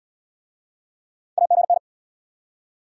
Morse code, heard as a keyed tone, about 76 wpm, 700 Hz. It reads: AGN